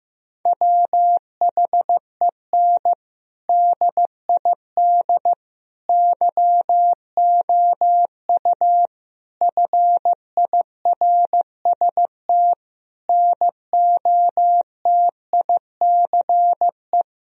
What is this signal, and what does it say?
Morse code, heard as a keyed tone, about 16 wpm, 700 Hz. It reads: WHEN DID YOU FIRST NOTICE
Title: Sentences from Top 1000 Words - Review Part 2 - 15wpm